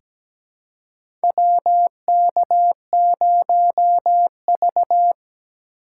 Morse code, heard as a keyed tone, about 17 wpm, 700 Hz. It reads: WK0V
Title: USA Call Signs - 4 Characters - 4x Character Spacing - 17wpm